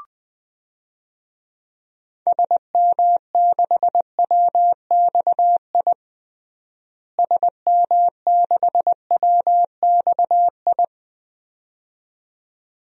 Morse code, heard as a keyed tone, about 20 wpm, 700 Hz. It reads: SM6WXI SM6WXI
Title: DXCC-100 Call Signs - 6 Characters - Repeated 2x - 20wpm